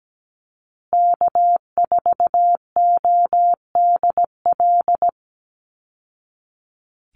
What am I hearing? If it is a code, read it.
Morse code, heard as a keyed tone, about 17 wpm, 700 Hz. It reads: K4ODL
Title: Call Signs Repeated 2x - 17wpm